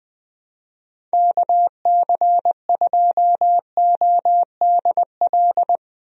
Morse code, heard as a keyed tone, about 20 wpm, 700 Hz. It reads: KC2ODL
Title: USA Call Signs - 6 Characters - Repeated 2x - 20wpm